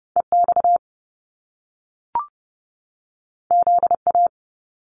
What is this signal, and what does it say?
Morse code, heard as a keyed tone, about 30 wpm, 700 Hz. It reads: EX ZA
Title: Top Two Letter Combinations in Words 30wpm